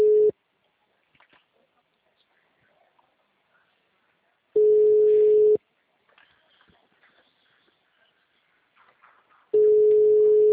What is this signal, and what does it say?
Ringback tone (Europe, 425 Hz).